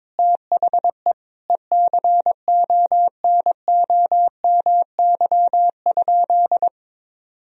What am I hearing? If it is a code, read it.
Morse code, heard as a keyed tone, about 22 wpm, 700 Hz. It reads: THE ECONOMY?
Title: Sentences from Top 1000 Words - Review Part 3 - 22wpm